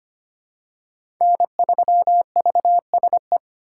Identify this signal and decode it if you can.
Morse code, heard as a keyed tone, about 25 wpm, 700 Hz. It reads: N3VSE
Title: USA Call Signs - 5 Characters - 2x Character Spacing - 25wpm